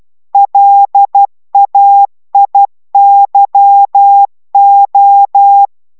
Morse code, heard as a keyed tone, about 12 wpm, 800 Hz. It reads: LAIYO